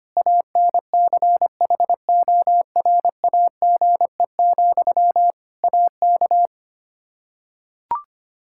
Morse code, heard as a keyed tone, about 25 wpm, 700 Hz. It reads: ANCHORAGE, AK
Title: Practice QSO - Exchange 1 - Beginning - 25wpm